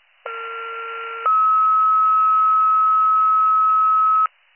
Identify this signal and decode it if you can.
Two-tone page, heard as a one-second tone, then a three-second tone, about 1370 then 1290 Hz. It is heard from 0.3 to 4.3 s.